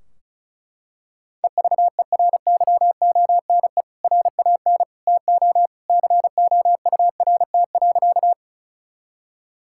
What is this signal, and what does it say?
Morse code, heard as a keyed tone, about 36 wpm, 700 Hz. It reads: EVERYONE RAN TO COURT.
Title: Sentences from Top 1600 Words 35wpm